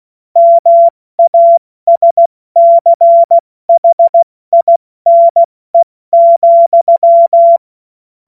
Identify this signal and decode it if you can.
Morse code, heard as a keyed tone, about 16 wpm, 680 Hz. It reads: MASCHINE,
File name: Morsekurs / Witze / Witz 16. Februar 2021, 70 F80